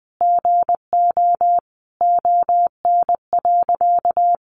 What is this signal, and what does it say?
Morse code, heard as a keyed tone, about 20 wpm, 700 Hz. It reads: GO ON.
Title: Sentences from Top 1300 Words 20wpm